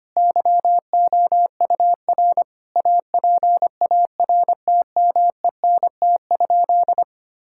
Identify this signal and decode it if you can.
Morse code, heard as a keyed tone, about 25 wpm, 700 Hz. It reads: YOUR APARTMENT?
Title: Sentences from Top 1000 Words - Review Part 6 - 25wpm